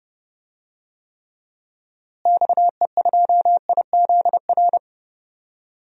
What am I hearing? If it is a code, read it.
Morse code, heard as a keyed tone, about 32 wpm, 700 Hz. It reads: XE2IZR
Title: DXCC-100 Call Signs - 6 Characters - Repeated 2x - 30wpm